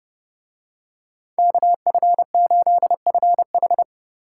Morse code, heard as a keyed tone, about 27 wpm, 700 Hz. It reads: KF8FH